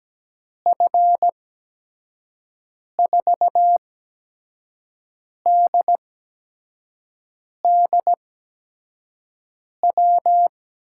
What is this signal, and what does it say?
Morse code, heard as a keyed tone, about 17 wpm, spaced as for 5 wpm, 700 Hz. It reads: F4DDW
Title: USA Call Signs - 6 Characters - 8x Character Spacing - 17wpm